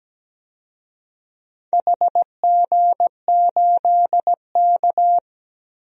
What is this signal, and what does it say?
Morse code, heard as a keyed tone, about 17 wpm, 700 Hz. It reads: HG8K
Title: DXCC-100 Call Signs - 4 Characters - Repeated 3x - 17wpm